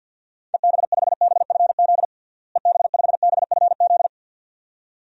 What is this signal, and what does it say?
Morse code, heard as a keyed tone, about 52 wpm, 700 Hz. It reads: EB5BFZ EB5BFZ
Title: DXCC-100 Call Signs - 6 Characters - Repeated 2x - 50wpm